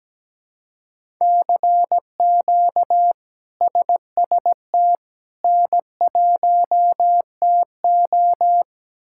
Morse code, heard as a keyed tone, about 17 wpm, 700 Hz. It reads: CQ SST N1TO
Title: K1USN SST QSO Practice - 17wpm